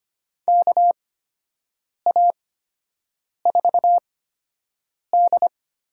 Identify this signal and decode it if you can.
Morse code, heard as a keyed tone, about 25 wpm, spaced as for 7 wpm, 700 Hz. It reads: KA4D
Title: USA Call Signs - 4 Characters - 8x Character Spacing - 25wpm